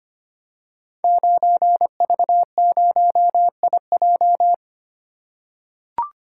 Morse code, heard as a keyed tone, about 25 wpm, 700 Hz. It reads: 9V0IJ
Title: Code Groups - 5 Letter:Numbers - 25wpm